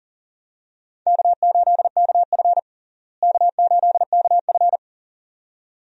Morse code, heard as a keyed tone, about 40 wpm, 700 Hz. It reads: K8KF K8KF
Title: USA Call Signs - 4 Characters - Repeated 2x - 40wpm